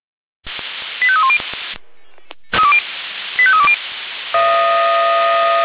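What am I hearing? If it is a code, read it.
Five-tone selective call (ZVEI), heard as quick runs of five tones, tones of about 70 ms. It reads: 85310 85320
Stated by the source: ZVEI five-tone call received on SDR.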